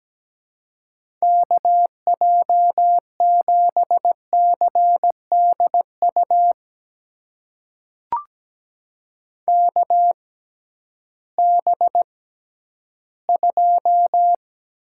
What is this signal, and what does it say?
Morse code, heard as a keyed tone, about 17 wpm, spaced as for 11 wpm, 700 Hz. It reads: KJ7CDU KB2
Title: USA Call Signs - 6 Characters - 6x Character Spacing - 17wpm